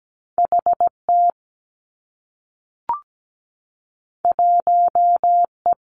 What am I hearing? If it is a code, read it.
Morse code, heard as a keyed tone, about 17 wpm, 700 Hz. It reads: HT 1E
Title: Two Challenging Letter-Number Combinations 17wpm